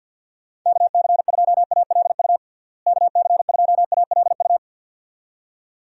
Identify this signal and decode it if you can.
Morse code, heard as a keyed tone, about 50 wpm, 700 Hz. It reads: KC2ALU KC2ALU
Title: USA Call Signs - 6 Characters - Repeated 2x - 50wpm